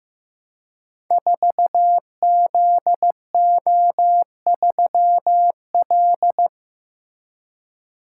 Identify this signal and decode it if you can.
Morse code, heard as a keyed tone, about 15 wpm, 700 Hz. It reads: 4ZO3L
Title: Code Groups - 5 Letter:Numbers - 15wpm